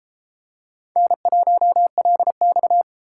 Morse code, heard as a keyed tone, about 31 wpm, 700 Hz. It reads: N1LX